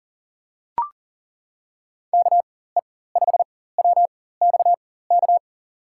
Morse code, heard as a keyed tone, about 39 wpm, spaced as for 18 wpm, 700 Hz. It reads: KE5WXK
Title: USA Call Signs - 6 Characters - 4x Character Spacing - 40wpm